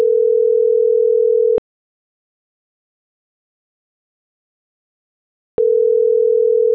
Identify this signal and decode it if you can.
Ringback tone (North America), 440 + 480 Hz.